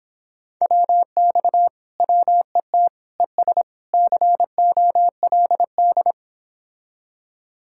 Morse code, heard as a keyed tone, about 26 wpm, 700 Hz. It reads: WX WET ES COLD